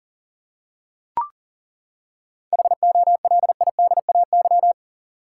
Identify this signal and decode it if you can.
Morse code, heard as a keyed tone, about 40 wpm, 700 Hz. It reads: HOLIDAY
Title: Seven Letter Words - 40wpm